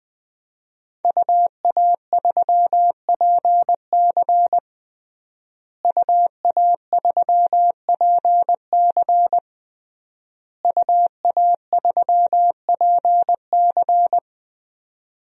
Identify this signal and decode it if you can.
Morse code, heard as a keyed tone, about 21 wpm, 700 Hz. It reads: UA3PC UA3PC UA3PC
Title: DXCC-100 Call Signs - 5 Characters - Repeated 3x - 20wpm